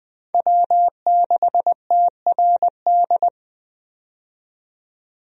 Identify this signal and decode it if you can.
Morse code, heard as a keyed tone, about 20 wpm, 700 Hz. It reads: W6TRD
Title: USA Call Signs - 5 Characters - 20wpm